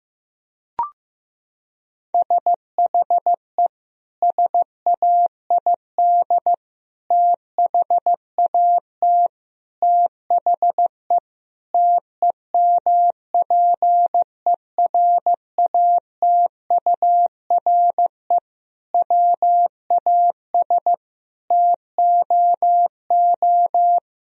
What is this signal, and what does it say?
Morse code, heard as a keyed tone, about 15 wpm, 700 Hz. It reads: SHE SAID THAT THE TEMPERATURE WAS TOO
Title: Sentences from Top 1000 Words - Review Part 4 - 15wpm